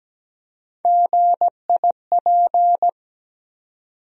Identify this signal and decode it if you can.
Morse code, heard as a keyed tone, about 17 wpm, 700 Hz. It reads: GIP
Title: USA Call Sign Postfixes - 3 Letters 17wpm